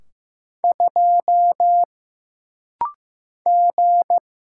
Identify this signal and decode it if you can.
Morse code, heard as a keyed tone, about 15 wpm, 700 Hz. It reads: 2 G
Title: Single Letter-Number Alt - Rapid-Fire 15wpm